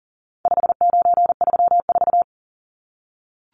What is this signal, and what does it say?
Morse code, heard as a keyed tone, about 40 wpm, 700 Hz. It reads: 5934
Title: Numbers - 4-Digit 40wpm